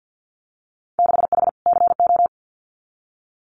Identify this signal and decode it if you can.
Morse code, heard as a keyed tone, about 50 wpm, 700 Hz. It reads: 6H CO
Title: Winter Field Day Exchanges Encore 50wpm